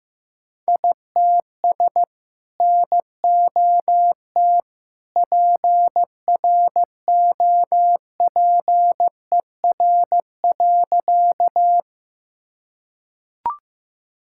Morse code, heard as a keyed tone, about 15 wpm, 700 Hz. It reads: ITS NOT PROPER.
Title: Sentences from Top 1000 Words - Review Part 5 - 15wpm